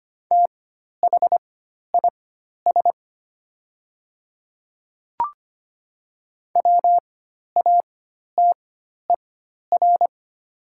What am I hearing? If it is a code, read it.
Morse code, heard as a keyed tone, about 25 wpm, spaced as for 12 wpm, 700 Hz. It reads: THIS WATER
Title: 4x Character Spacing - Top 100 Words - 25wpm